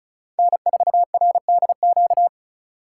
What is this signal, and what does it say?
Morse code, heard as a keyed tone, about 33 wpm, 700 Hz. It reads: N4RDQ
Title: Speed Racing - USA Call Signs 35wpm